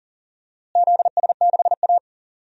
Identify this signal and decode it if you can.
Morse code, heard as a keyed tone, about 40 wpm, 700 Hz. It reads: ZS6A
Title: DXCC-100 Call Signs - 4 Characters - Repeated 3x - 40wpm